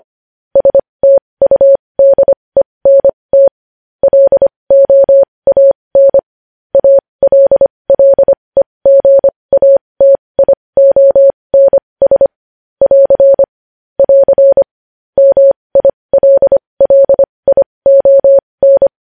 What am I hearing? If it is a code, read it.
Morse code, heard as a keyed tone, about 25 wpm, spaced as for 20 wpm, 550 Hz. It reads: STUDENT LOAN ALLEGATIONS <AR> <AR> MILLION
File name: podcast / News headlines-1x-25x20